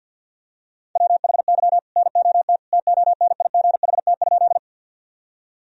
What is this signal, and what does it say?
Morse code, heard as a keyed tone, about 52 wpm, 700 Hz. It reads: WHY NOT TONIGHT?
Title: Sentences from Top 1000 Words - Review Part 3 - 50wpm